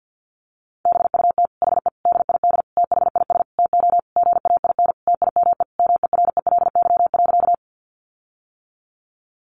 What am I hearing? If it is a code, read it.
Morse code, heard as a keyed tone, about 50 wpm, 700 Hz. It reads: BUT HE DID THIS TO GAIN TIME MERELY.
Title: Sentences from Top 1400 Words 50wpm